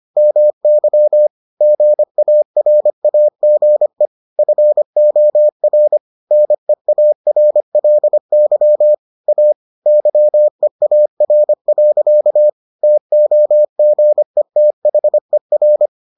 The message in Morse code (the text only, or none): MY GARAGE FOR NEARLY A YEAR. TOGETHER